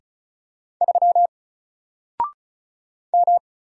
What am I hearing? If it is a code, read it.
Morse code, heard as a keyed tone, about 33 wpm, 700 Hz. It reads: 3 M
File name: Single-Letter-Number-35wpm